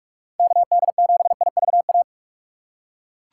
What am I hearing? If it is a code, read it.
Morse code, heard as a keyed tone, about 45 wpm, 700 Hz. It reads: KD7IVA
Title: USA Call Signs - 6x Character Spacing - 45wpm